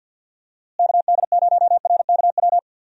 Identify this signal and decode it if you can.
Morse code, heard as a keyed tone, about 49 wpm, 700 Hz. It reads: KD0RKW
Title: USA Call Signs - 6 Characters - Repeated 3x - 50wpm